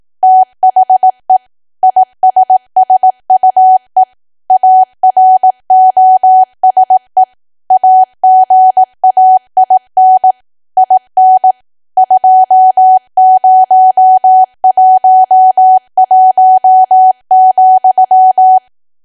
Morse code, heard as a keyed tone, about 18 wpm, 750 Hz. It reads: THE ISSUE AROSE AGAIN IN 2011,
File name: cwa / W1AW-Level2 / 18WPM